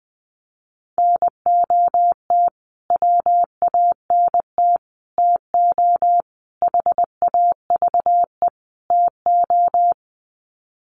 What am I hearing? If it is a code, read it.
Morse code, heard as a keyed tone, about 20 wpm, 700 Hz. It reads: NOT WANT TO HAVE TO
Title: Sets of 5 Words Encore2 20wpm